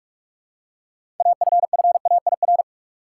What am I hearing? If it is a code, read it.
Morse code, heard as a keyed tone, about 46 wpm, 700 Hz. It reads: AFFAIR
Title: Six Letter Words - 45wpm